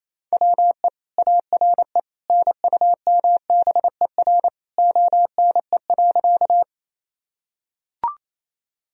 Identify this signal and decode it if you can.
Morse code, heard as a keyed tone, about 28 wpm, 700 Hz. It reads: WE ARE NUMBER ONE.
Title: Sentences from Top 1000 Words - Review Part 1 - 28wpm